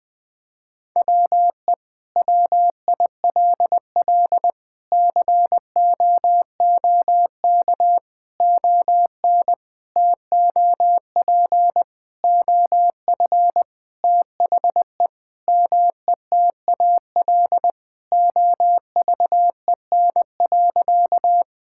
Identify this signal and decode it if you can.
Morse code, heard as a keyed tone, about 20 wpm, 700 Hz. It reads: WE WILL COOK ON TOP OF THE METAL OVEN.
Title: Sentences from Top 1000 Words - Review Part 6 - 20wpm